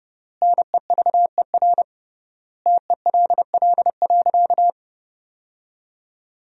Morse code, heard as a keyed tone, about 30 wpm, 700 Hz. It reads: NEVER TELL.
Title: 3x Word-Spacing - Sentences from Top 300 Words - 30wpm